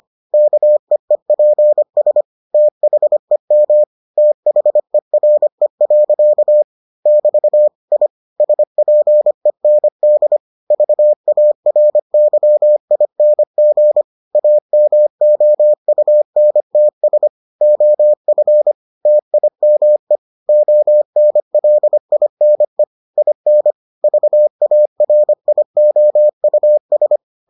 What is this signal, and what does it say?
Morse code, heard as a keyed tone, about 25 wpm, 600 Hz. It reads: KEEPS THEM THERE. <BT> I SPEND VARYING AMOUNTS OF TIME ONLINE IN VARIOUS